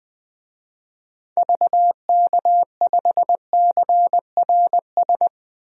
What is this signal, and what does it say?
Morse code, heard as a keyed tone, about 20 wpm, 700 Hz. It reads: VK5CRS
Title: DXCC-100 Call Signs - 6 Characters - 20wpm